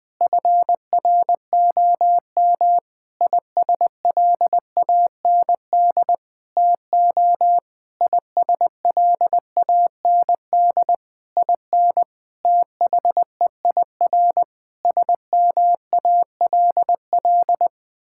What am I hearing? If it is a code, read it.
Morse code, heard as a keyed tone, about 20 wpm, 700 Hz. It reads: FROM ISLAND TO ISLAND IN THEIR SMALL
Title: Sentences from Top 1000 Words - Review Part 6 - 20wpm